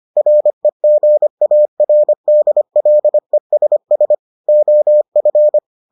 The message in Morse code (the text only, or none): REGARDLESS OF